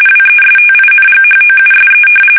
RTTY, about 45.45 baud. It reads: WESTINGHOUSE
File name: fsk850 / battleshipnj